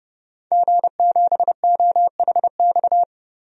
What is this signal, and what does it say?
Morse code, heard as a keyed tone, about 32 wpm, 700 Hz. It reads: G7OHX
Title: DXCC-100 Call Signs - 5 Characters - 30wpm